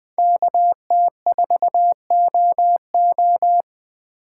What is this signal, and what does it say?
Morse code, heard as a keyed tone, about 20 wpm, 700 Hz. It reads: KT4OO
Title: USA Call Signs - 5 Characters - 4x Character Spacing - 20wpm